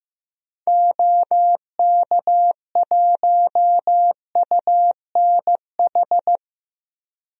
Morse code, heard as a keyed tone, about 16 wpm, 700 Hz. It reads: OK1UNH
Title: DXCC-100 Call Signs - 6 Characters - 15wpm